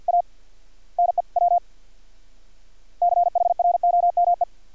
Morse code, heard as a keyed tone, about 50 wpm, 700 Hz. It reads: A NEW CHROME
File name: archived / CRSnet